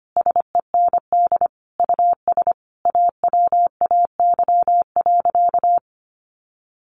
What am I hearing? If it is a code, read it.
Morse code, heard as a keyed tone, about 25 wpm, 700 Hz. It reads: SEND US AWAY.
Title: Sentences from Top 1500 Words 25wpm